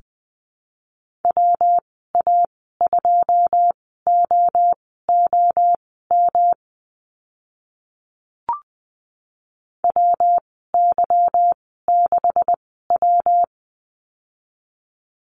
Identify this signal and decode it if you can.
Morse code, heard as a keyed tone, about 20 wpm, spaced as for 14 wpm, 700 Hz. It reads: WA2OOM WY6W
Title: 161 Lesson 26g - Letter g - Call Signs 20wpm